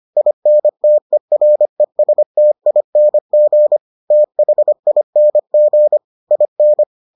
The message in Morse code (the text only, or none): INTERESTING THING IN